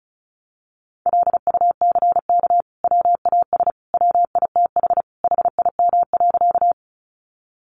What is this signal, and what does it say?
Morse code, heard as a keyed tone, about 34 wpm, 700 Hz. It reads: LUCK WAS WITH HIM.
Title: Sentences from Top 1500 Words 35wpm